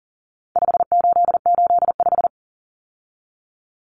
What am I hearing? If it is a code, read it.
Morse code, heard as a keyed tone, about 40 wpm, 700 Hz. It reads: 5885
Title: Numbers - 4-Digit 40wpm